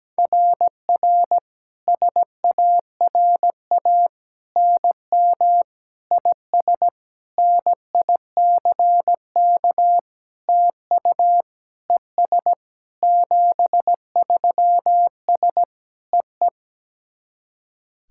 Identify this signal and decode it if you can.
Morse code, heard as a keyed tone, about 17 wpm, 700 Hz. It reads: RR SARA NM IS NICK TU ES 73S EE
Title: SOTA QSO Practice 17wpm